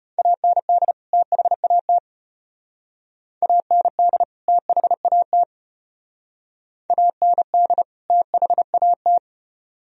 Morse code, heard as a keyed tone, about 35 wpm, 700 Hz. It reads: AND THAT AND THAT AND THAT